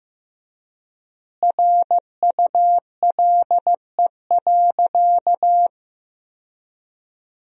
Morse code, heard as a keyed tone, about 15 wpm, 700 Hz. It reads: RULE.
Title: 4x Word-Spacing - Sentences from Top 400 Words - 15wpm